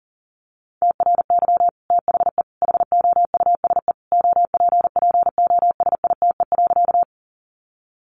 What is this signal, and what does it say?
Morse code, heard as a keyed tone, about 40 wpm, 700 Hz. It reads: TRY THE HOUSE OPPOSITE.
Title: Sentences from Top 1300 Words 40wpm